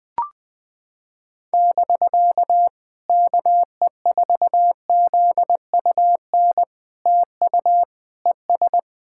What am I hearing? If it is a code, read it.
Morse code, heard as a keyed tone, about 20 wpm, 700 Hz. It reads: <BK> KE4ZUN TU ES
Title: POTA QSO Practice - 20wpm